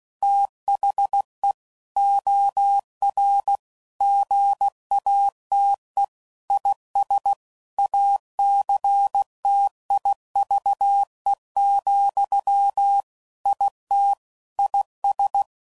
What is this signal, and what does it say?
Morse code, heard as a keyed tone, about 16 wpm, 800 Hz. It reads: THE OR GATE IS ACTIVE, IT IS
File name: CW Code Groups @15WPM / Technical Subjects / AND Gate